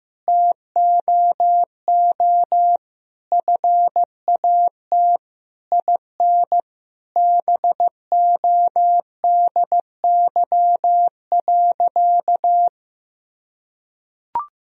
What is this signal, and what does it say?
Morse code, heard as a keyed tone, about 15 wpm, 700 Hz. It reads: TOO FAT IN BODY.
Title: Sentences from Top 1000 Words - Review Part 4 - 15wpm